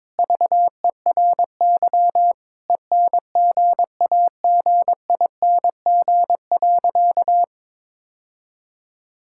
Morse code, heard as a keyed tone, about 22 wpm, 700 Hz. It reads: VERY ENGAGING.
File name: practice / 22wpm / Sentences with 2-Word Phrases - 22wpm